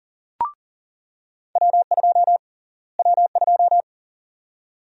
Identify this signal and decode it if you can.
Morse code, heard as a keyed tone, about 40 wpm, 700 Hz. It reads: W2 W2
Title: USA Call Sign Prefixes - Repeated 2x - 40wpm